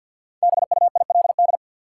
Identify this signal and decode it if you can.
Morse code, heard as a keyed tone, about 53 wpm, 700 Hz. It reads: BUILD
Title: Sentences from Top 1000 Words - Review Part 1 - New Word Drill - 50wpm